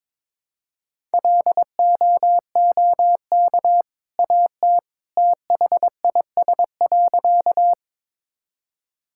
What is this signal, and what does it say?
Morse code, heard as a keyed tone, about 22 wpm, 700 Hz. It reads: LOOK AT THIS.
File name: practice / 22wpm / Sentences with 2-Word Phrases - 22wpm